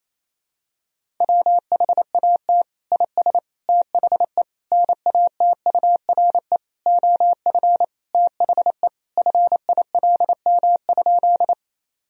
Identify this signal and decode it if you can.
Morse code, heard as a keyed tone, about 27 wpm, 700 Hz. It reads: WHAT IS THE NATURE OF THE FILM?
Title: Sentences from Top 1000 Words - Review Part 3 - 28wpm